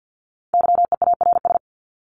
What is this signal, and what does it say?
Morse code, heard as a keyed tone, about 50 wpm, 700 Hz. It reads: YEARS
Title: ES:S Suffix - Top 100 Words with Base Words 50wpm